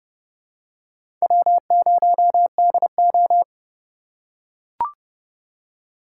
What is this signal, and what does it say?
Morse code, heard as a keyed tone, about 30 wpm, 700 Hz. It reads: W0DO
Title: USA Call Signs - 4 Characters - 2x Character Spacing - 30wpm